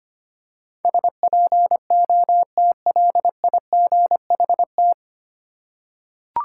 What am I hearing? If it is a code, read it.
Morse code, heard as a keyed tone, about 25 wpm, 700 Hz. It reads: SPOTLIGHT
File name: Nine Letter Words - 25wpm